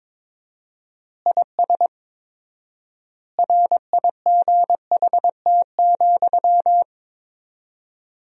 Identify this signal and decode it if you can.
Morse code, heard as a keyed tone, about 22 wpm, 700 Hz. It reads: IS RIGHT,
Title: 4x Word-Spacing - Sentences from Top 200 Words - 22wpm